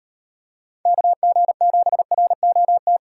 Morse code, heard as a keyed tone, about 38 wpm, 700 Hz. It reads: KG7ROT